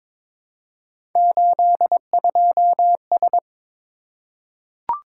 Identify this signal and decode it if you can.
Morse code, heard as a keyed tone, about 22 wpm, 700 Hz. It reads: 82S
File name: practice / 22wpm / Three Letter-Number Combinations - 22wpm